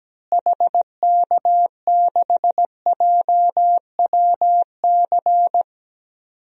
Morse code, heard as a keyed tone, about 17 wpm, 700 Hz. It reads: HK6JWC
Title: DXCC-100 Call Signs - 6 Characters - Repeated 3x - 17wpm